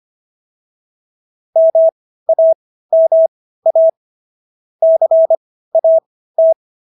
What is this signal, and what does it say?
Morse code, heard as a keyed tone, about 25 wpm, spaced as for 15 wpm, 650 Hz. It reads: MAMA CAT